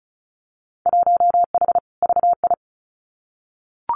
Morse code, heard as a keyed tone, about 34 wpm, 700 Hz. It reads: 1H VI
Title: Winter Field Day Exchanges Encore 35wpm